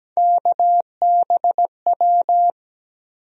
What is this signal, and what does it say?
Morse code, heard as a keyed tone, about 17 wpm, 700 Hz. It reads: KBW
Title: USA Call Sign Postfixes - 3 Letters 17wpm